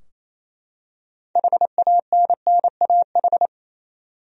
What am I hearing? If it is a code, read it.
Morse code, heard as a keyed tone, about 28 wpm, 700 Hz. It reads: HANNAH
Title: USA Common Names - 28wpm